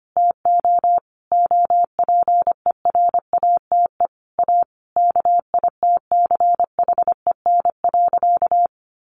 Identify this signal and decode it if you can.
Morse code, heard as a keyed tone, about 25 wpm, 700 Hz. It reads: TO OPERATE A KITCHEN.
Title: Sentences from Top 1500 Words 25wpm